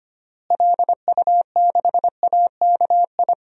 Morse code, heard as a keyed tone, about 25 wpm, 700 Hz. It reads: LU6AKI
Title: DXCC-100 Call Signs - 6 Characters - Repeated 3x - 25wpm